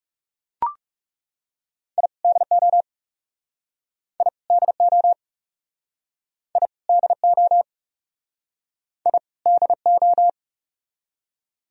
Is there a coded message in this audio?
Yes: Morse code from 2.0 to 10.3 s, heard as a keyed tone, about 41 wpm, 700 Hz. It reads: I DO I DO I DO I DO